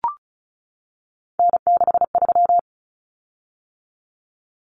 Morse code, heard as a keyed tone, about 34 wpm, 700 Hz. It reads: N63
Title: Common Cut Numbers - 3-Digit 35wpm